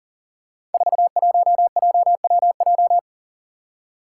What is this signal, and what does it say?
Morse code, heard as a keyed tone, about 38 wpm, 700 Hz. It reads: 41JWJ
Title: Five Challenging Letter-Number Combinations 40wpm